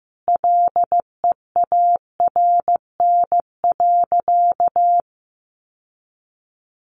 Morse code, heard as a keyed tone, about 15 wpm, 700 Hz. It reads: LEARN.
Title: Sentences from Top 1400 Words 15wpm